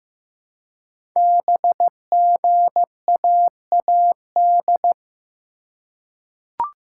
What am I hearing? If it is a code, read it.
Morse code, heard as a keyed tone, about 15 wpm, 700 Hz. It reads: BGAAD
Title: Code Groups - 5 Letters - 15wpm